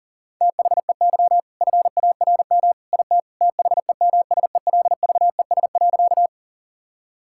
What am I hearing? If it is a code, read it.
Morse code, heard as a keyed tone, about 40 wpm, 700 Hz. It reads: THEY FARM IT THEMSELVES.
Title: Sentences from Top 1000 Words - Review Part 5 - 40wpm